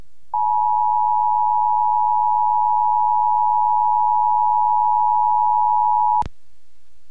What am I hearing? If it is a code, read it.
Emergency Alert System (SAME) tones (the 853 and 960 Hz attention tone). It is heard from 0.3 to 6.3 s.